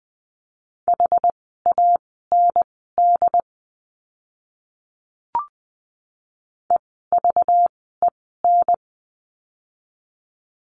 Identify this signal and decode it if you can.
Morse code, heard as a keyed tone, about 20 wpm, spaced as for 14 wpm, 700 Hz. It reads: HAND EVEN
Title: 237 Lesson 38f - Letter q - Words 20wpm